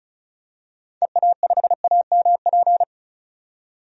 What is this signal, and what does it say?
Morse code, heard as a keyed tone, about 34 wpm, 700 Hz. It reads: EA5AMP
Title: DXCC-100 Call Signs - 6 Characters - 35wpm